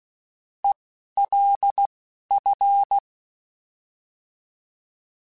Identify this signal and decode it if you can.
Morse code, heard as a keyed tone, about 16 wpm, spaced as for 11 wpm, 800 Hz. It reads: ELF
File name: Alphabetical / Alphabetical mp3s / Es @10